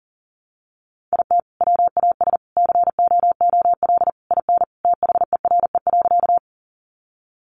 Morse code, heard as a keyed tone, about 40 wpm, 700 Hz. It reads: IT WAS COOL IN THERE.
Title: Sentences from Top 1200 Words 40wpm